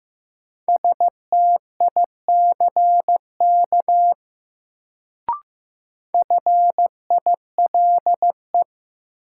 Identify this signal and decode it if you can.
Morse code, heard as a keyed tone, about 15 wpm, 700 Hz. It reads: STICK FILE
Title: Sentences from Top 1200 Words - New Word Drill Alt - 15wpm